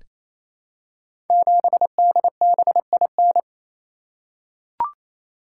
Morse code, heard as a keyed tone, about 27 wpm, 700 Hz. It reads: 7DBIN
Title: Code Groups - 5 Letter:Numbers - 28wpm